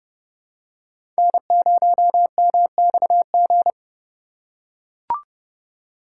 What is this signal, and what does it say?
Morse code, heard as a keyed tone, about 30 wpm, 700 Hz. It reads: N0MXG
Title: USA Call Signs - 5 Characters - 2x Character Spacing - 30wpm